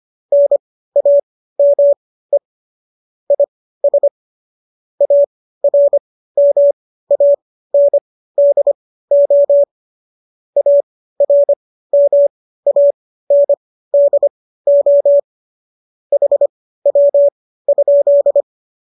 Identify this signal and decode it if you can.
Morse code, heard as a keyed tone, about 25 wpm, spaced as for 15 wpm, 570 Hz. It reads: NAME IS ARMANDO ARMANDO HW?